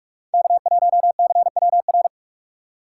Morse code, heard as a keyed tone, about 44 wpm, 700 Hz. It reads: K1CWR